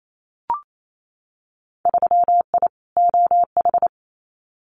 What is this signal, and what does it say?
Morse code, heard as a keyed tone, about 28 wpm, 700 Hz. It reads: 3I OH
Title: Winter Field Day Exchanges Encore 28wpm